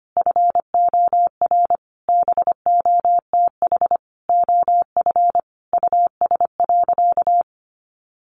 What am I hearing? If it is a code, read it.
Morse code, heard as a keyed tone, about 25 wpm, 700 Hz. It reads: FOR BOTH OF US.
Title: Sentences from Top 1100 Words 25wpm